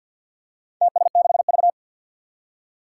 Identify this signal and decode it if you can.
Morse code, heard as a keyed tone, about 50 wpm, 700 Hz. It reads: TS6V
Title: Four Challenging Letter-Number Combinations 50wpm